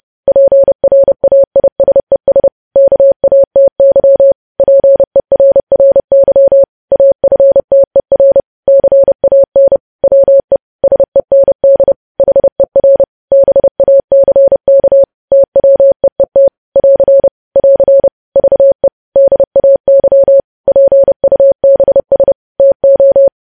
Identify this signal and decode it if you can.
Morse code, heard as a keyed tone, about 30 wpm, 550 Hz. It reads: PRAISES KATY PERRY AFTER CAN WE SEND HER BACK TWEET <AR> <AR> VE DAY PUBS TO
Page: News Headlines in Morse Code at 30 WPM